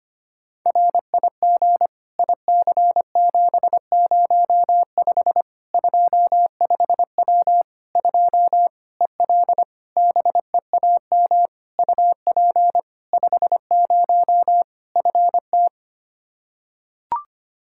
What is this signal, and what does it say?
Morse code, heard as a keyed tone, about 25 wpm, 700 Hz. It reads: RIG IC705 25W 2 EL BEAM UP 50 FT
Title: Practice QSO - Exchange 2 - Info - 25wpm